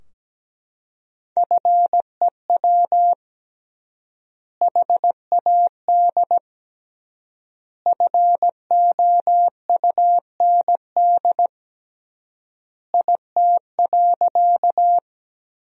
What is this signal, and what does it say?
Morse code, heard as a keyed tone, about 17 wpm, 700 Hz. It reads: FEW HAD FOUND IT.
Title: 3x Word-Spacing - Sentences from Top 300 Words - 17wpm